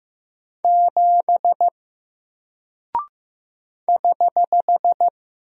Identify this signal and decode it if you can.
Morse code, heard as a keyed tone, about 15 wpm, 700 Hz. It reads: 7 <HH>
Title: Single Character (Letter:Number:Punctuation:Prosign) Alt 15wpm